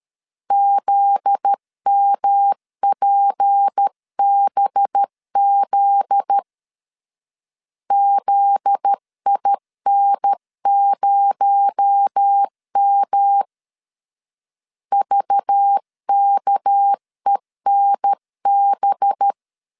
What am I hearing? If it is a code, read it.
Morse code, heard as a keyed tone, about 13 wpm, 800 Hz. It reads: ZMPBZ ZIN0M VKENB
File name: LEZIONI / 7 GF9Z / 7 GF9Z 12wpm 2147